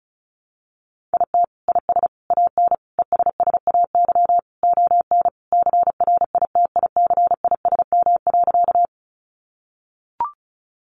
Morse code, heard as a keyed tone, about 34 wpm, 700 Hz. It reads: IT IS AN ESSAY ON CRITICISM.
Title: Sentences from Top 1500 Words 35wpm